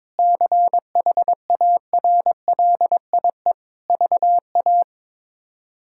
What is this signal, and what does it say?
Morse code, heard as a keyed tone, about 22 wpm, 700 Hz. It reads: CHARLIE VA
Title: K1USN SST QSO Practice - 22wpm